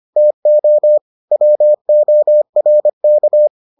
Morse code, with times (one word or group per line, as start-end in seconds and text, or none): 0.2-1.0 TO
1.3-3.5 WORK